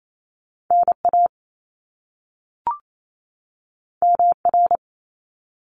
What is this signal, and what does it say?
Morse code, heard as a keyed tone, about 28 wpm, 700 Hz. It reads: NA MR
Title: Top Two Letter Combinations in Words 28wpm